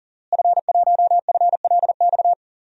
Morse code, heard as a keyed tone, about 40 wpm, 700 Hz. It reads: F1FLX
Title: DXCC-100 Call Signs - 5 Characters - Repeated 3x - 40wpm